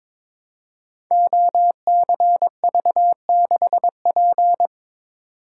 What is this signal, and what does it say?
Morse code, heard as a keyed tone, about 22 wpm, 700 Hz. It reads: OCV6P
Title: Code Groups - 5 Letter:Numbers - 22wpm